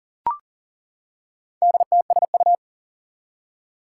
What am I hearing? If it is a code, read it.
Morse code, heard as a keyed tone, about 38 wpm, 700 Hz. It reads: DTSU
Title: Four Challenging Letter-Number Combinations 40wpm